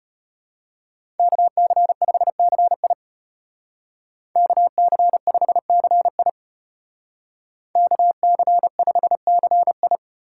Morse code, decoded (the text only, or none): KC5CI KC5CI KC5CI